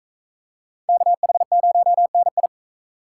Morse code, heard as a keyed tone, about 42 wpm, 700 Hz. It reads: KH0NI